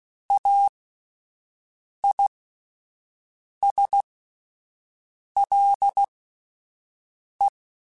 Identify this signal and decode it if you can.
Morse code, heard as a keyed tone, about 16 wpm, spaced as for 5 wpm, 800 Hz. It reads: AISLE
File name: Alphabetical / Alphabetical mp3s / As @5